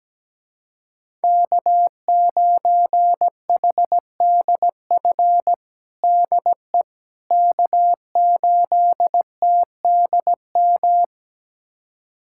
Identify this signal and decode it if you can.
Morse code, heard as a keyed tone, about 17 wpm, 700 Hz. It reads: K9HDF DE K8TDM
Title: USA Callsign DE Callsign - 5 Characters - 17wpm